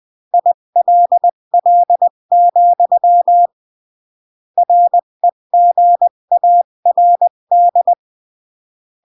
Morse code, heard as a keyed tone, about 20 wpm, 700 Hz. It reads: ILL, REGARD
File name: CW / jodorowsky / 82 commandments 20-15